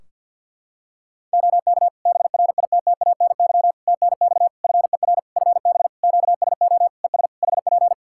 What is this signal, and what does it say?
Morse code, heard as a keyed tone, about 51 wpm, 700 Hz. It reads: OK BRITTANY TNX FER FB QSO ES HP
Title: Practice QSO - Exchange 4 - Ending - 50wpm